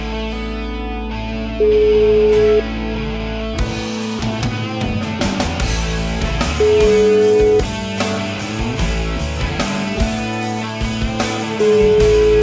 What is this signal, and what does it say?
Ringback tone (Europe, 425 Hz), with noise.